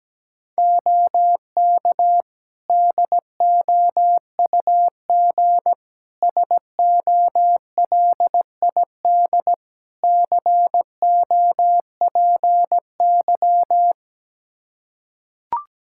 Morse code, heard as a keyed tone, about 17 wpm, 700 Hz. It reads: OK DOUG SOLID COPY
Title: Practice QSO - Exchange 3 - More Info - 17wpm